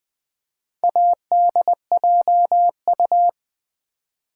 Morse code, heard as a keyed tone, about 20 wpm, 700 Hz. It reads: ADJU
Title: Build A Word - 2 Letters At A Time - 6-Letter Words - 20wpm